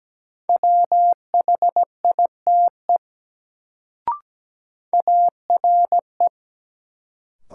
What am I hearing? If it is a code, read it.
Morse code, heard as a keyed tone, about 17 wpm, 700 Hz. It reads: WHITE ARE
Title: Sets of 2 Words - New Word Drill - 17wpm